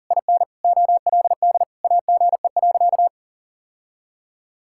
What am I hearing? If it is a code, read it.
Morse code, heard as a keyed tone, about 40 wpm, 700 Hz. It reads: IN OLD AGE.
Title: Sentences from Top 1000 Words - Review Part 6 - 40wpm